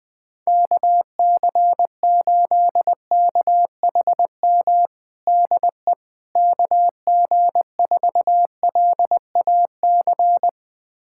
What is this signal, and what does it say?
Morse code, heard as a keyed tone, about 20 wpm, 700 Hz. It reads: KC8KHM DE KG4LAC
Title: USA Callsign DE Callsign - 6 Characters - 20wpm